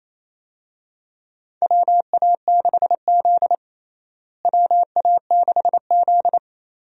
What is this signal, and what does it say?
Morse code, heard as a keyed tone, about 27 wpm, 700 Hz. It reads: WA6Z WA6Z
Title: USA Call Signs - 4 Characters - Repeated 3x - 28wpm